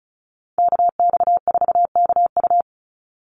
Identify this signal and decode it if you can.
Morse code, heard as a keyed tone, about 34 wpm, 700 Hz. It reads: KX4KU
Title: Call Signs Repeated 2x - 35wpm